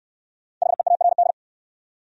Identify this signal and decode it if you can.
Morse code, heard as a keyed tone, about 68 wpm, 700 Hz. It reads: HEARD